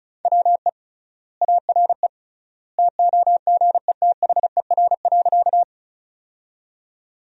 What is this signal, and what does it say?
Morse code, heard as a keyed tone, about 36 wpm, 700 Hz. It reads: WE ARE TOGETHER.
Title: 3x Word-Spacing - Sentences from Top 300 Words - 35wpm